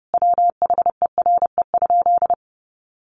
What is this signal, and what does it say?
Morse code, heard as a keyed tone, about 30 wpm, 700 Hz. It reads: WHERE?
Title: Sentences from Top 1200 Words 30wpm